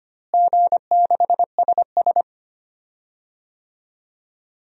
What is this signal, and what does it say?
Morse code, heard as a keyed tone, about 25 wpm, 700 Hz. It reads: G6SS
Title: DXCC-100 Call Signs - 4 Characters - Repeated 3x - 25wpm